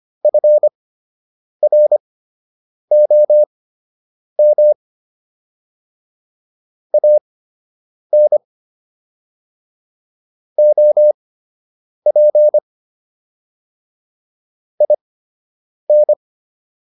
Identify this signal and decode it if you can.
Morse code, heard as a keyed tone, about 25 wpm, spaced as for 8 wpm, 600 Hz. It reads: FROM AN OP IN